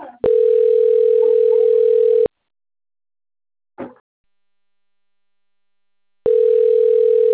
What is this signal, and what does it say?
Ringback tone (North America), 440 + 480 Hz.